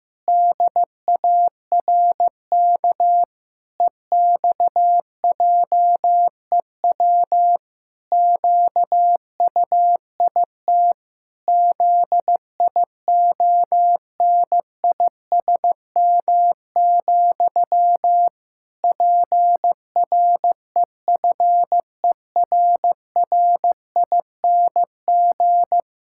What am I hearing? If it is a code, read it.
Morse code, heard as a keyed tone, about 15 wpm, 700 Hz. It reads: DARK EXJEW QUIT ZIONISM, PREFERRING
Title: Pangram Challenge v2 - 15wpm